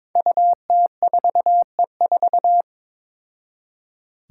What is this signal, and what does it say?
Morse code, heard as a keyed tone, about 22 wpm, 700 Hz. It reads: UT4E4
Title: Five Challenging Letter-Number Combinations 22wpm